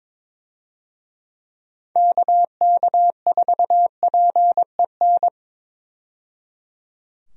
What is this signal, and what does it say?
Morse code, heard as a keyed tone, about 22 wpm, 700 Hz. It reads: KK4PEN
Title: USA Call Signs - 6 Characters - 22wpm